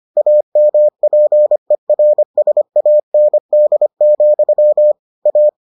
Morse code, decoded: AMPERSAND, A

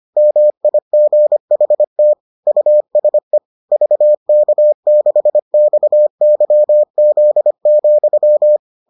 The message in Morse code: MIGHT USE VK6XYZ,

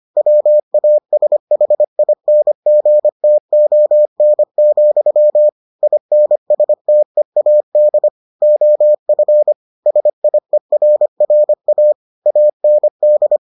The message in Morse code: WASHINGTON, INSTEAD OF SIERRA AND

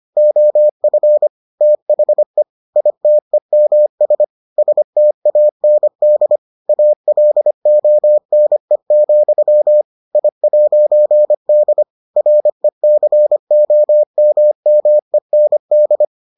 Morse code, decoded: OF THE ITEMS STAND ALONE, I'D RECOMMEND

